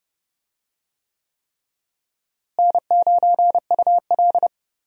Morse code, heard as a keyed tone, about 30 wpm, 700 Hz. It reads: N9UL
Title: USA Call Signs - 4 Characters - Repeated 3x - 30wpm